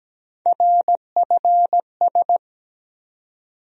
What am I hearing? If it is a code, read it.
Morse code, heard as a keyed tone, about 17 wpm, 700 Hz. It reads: RFS
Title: USA Call Sign Postfixes - 3 Letters 17wpm